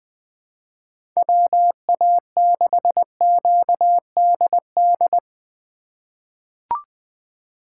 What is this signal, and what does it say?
Morse code, heard as a keyed tone, about 21 wpm, 700 Hz. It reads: WA6QDD
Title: USA Call Signs - 6 Characters - 3x Character Spacing - 20wpm